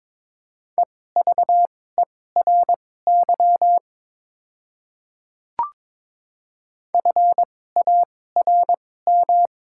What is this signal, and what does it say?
Morse code, heard as a keyed tone, about 22 wpm, spaced as for 16 wpm, 700 Hz. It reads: EVERY FARM
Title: 2x Character Spacing - Top 500 Words - 22wpm